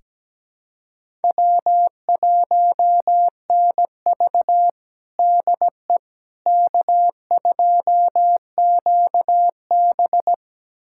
Morse code, heard as a keyed tone, about 17 wpm, 700 Hz. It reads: W1NV DE K2QB
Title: USA Callsign DE Callsign - 4 Characters - 17wpm